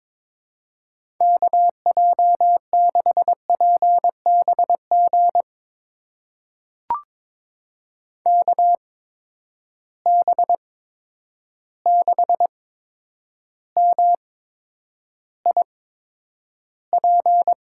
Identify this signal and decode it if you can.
Morse code, heard as a keyed tone, about 22 wpm, spaced as for 9 wpm, 700 Hz. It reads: KJ6PBG KB6MIP
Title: USA Call Signs - 6 Characters - 8x Character Spacing - 22wpm